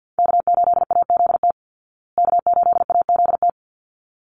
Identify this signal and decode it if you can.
Morse code, heard as a keyed tone, about 50 wpm, 700 Hz. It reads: K8AZT K8AZT
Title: Call Signs Repeated 3x - 50wpm